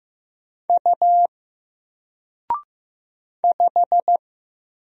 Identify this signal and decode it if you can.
Morse code, heard as a keyed tone, about 15 wpm, 700 Hz. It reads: U 5
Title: Single Letter-Number 15wpm